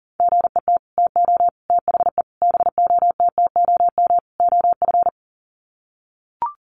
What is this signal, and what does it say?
Morse code, heard as a keyed tone, about 40 wpm, 700 Hz. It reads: GET TO THE BOTTOM OF
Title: Sets of 5 Words Encore 40wpm